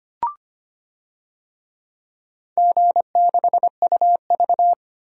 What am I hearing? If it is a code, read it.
Morse code, heard as a keyed tone, about 25 wpm, 700 Hz. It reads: G6UV
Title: DXCC-100 Call Signs - 4 Characters - 25wpm